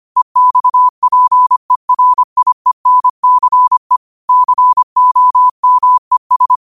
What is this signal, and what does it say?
Morse code, heard as a keyed tone, about 25 wpm, 1000 Hz. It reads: EXPERIENCE COMES